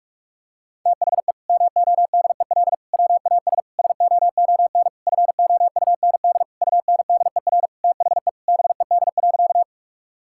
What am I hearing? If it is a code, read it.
Morse code, heard as a keyed tone, about 46 wpm, 700 Hz. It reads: THE MODEL WAS SOON FOUND UNDER THE BED.
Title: Sentences from Top 1000 Words - Review Part 5 - 45wpm